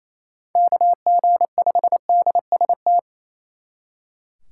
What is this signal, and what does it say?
Morse code, heard as a keyed tone, about 35 wpm, 700 Hz. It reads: KG5DST
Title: Speed Racing - USA 2x3 Call Signs 28wpm